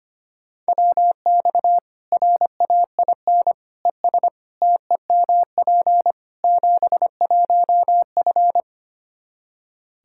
Morse code, heard as a keyed tone, about 25 wpm, 700 Hz. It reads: WX RAIN ES TEMP 71F
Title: K1USN SST QSO Practice 25wpm